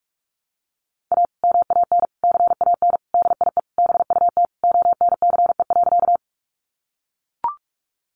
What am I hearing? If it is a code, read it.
Morse code, heard as a keyed tone, about 44 wpm, 700 Hz. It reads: A MAN CAN DIE BUT ONCE.
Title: Sentences from Top 1300 Words 45wpm